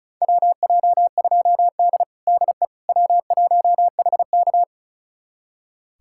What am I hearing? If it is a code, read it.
Morse code, heard as a keyed tone, about 36 wpm, 700 Hz. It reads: WJ2D DE W1HK
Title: USA Callsign DE Callsign - 4 Characters - 35wpm